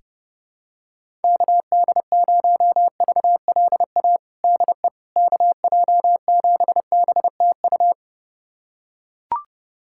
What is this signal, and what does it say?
Morse code, heard as a keyed tone, about 30 wpm, 700 Hz. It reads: KD0VLA DE KJ7BTU
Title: USA Callsign DE Callsign - 6 Characters - 30wpm